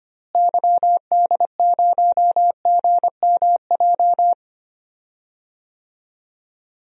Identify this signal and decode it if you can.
Morse code, heard as a keyed tone, about 25 wpm, 700 Hz. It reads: YD0GMJ
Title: DXCC-100 Call Signs - 6 Characters - Repeated 2x - 25wpm